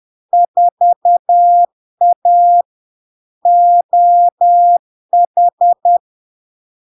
Morse code, heard as a keyed tone, about 10 wpm, 700 Hz. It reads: 4A OH
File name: LARK-FD-2022-Log-in-CW-10wpm-700-Hz